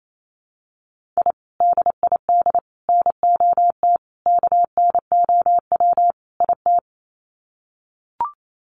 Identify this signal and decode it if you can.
Morse code, heard as a keyed tone, about 28 wpm, 700 Hz. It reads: I DID NOT KNOW IT
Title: Sets of 5 Words Encore2 28wpm